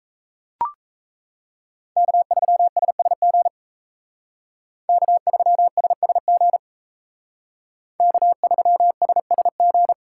Morse code, heard as a keyed tone, about 34 wpm, 700 Hz. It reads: K3SSG K3SSG K3SSG
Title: Speed Racing - USA 1x3 Call Signs 28wpm